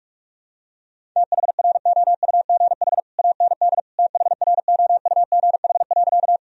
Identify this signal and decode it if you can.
Morse code, heard as a keyed tone, about 46 wpm, 700 Hz. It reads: THROUGH AND THROUGH.
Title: Sentences from Top 1000 Words - Review Part 1 - 45wpm